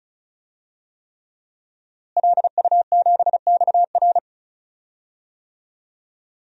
Morse code, heard as a keyed tone, about 34 wpm, 700 Hz. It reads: LU7XR
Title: DXCC-100 Call Signs - 5 Characters - Repeated 3x - 35wpm